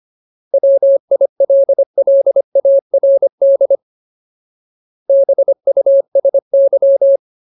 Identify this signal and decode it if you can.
Morse code, heard as a keyed tone, about 25 wpm, 550 Hz. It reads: WILLARD BUSY
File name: Ultimate Word List / 25WPM Ultimate